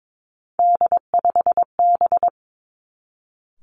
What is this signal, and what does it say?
Morse code, heard as a keyed tone, about 22 wpm, 700 Hz. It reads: D5B
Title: Three Challenging Letter-Number Combinations 22wpm